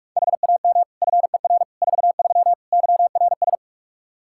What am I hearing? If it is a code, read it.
Morse code, heard as a keyed tone, about 46 wpm, 700 Hz. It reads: HAM FER 43 YRS
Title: K1USN SST QSO Practice 45wpm